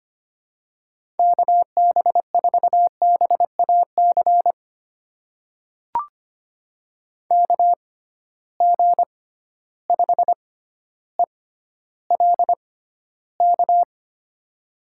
Morse code, heard as a keyed tone, about 25 wpm, spaced as for 13 wpm, 700 Hz. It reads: KB4BAC KG5ELK
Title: USA Call Signs - 6 Characters - 6x Character Spacing - 25wpm